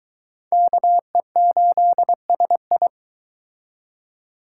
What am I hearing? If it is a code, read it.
Morse code, heard as a keyed tone, about 22 wpm, 700 Hz. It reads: KE8SI